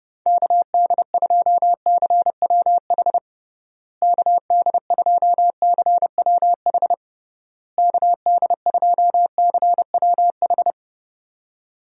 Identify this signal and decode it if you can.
Morse code, heard as a keyed tone, about 30 wpm, 700 Hz. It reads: KD2CWH KD2CWH KD2CWH
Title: USA Call Signs - 6 Characters - Repeated 3x - 30wpm